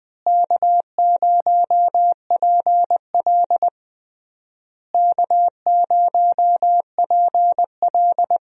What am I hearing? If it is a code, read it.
Morse code, heard as a keyed tone, about 20 wpm, 700 Hz. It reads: K0PL K0PL
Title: USA Call Signs - 4 Characters - Repeated 3x - 20wpm